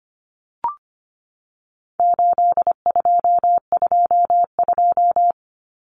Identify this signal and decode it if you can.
Morse code, heard as a keyed tone, about 25 wpm, 700 Hz. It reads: 8222
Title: Common Cut Numbers - 4-Digit 25wpm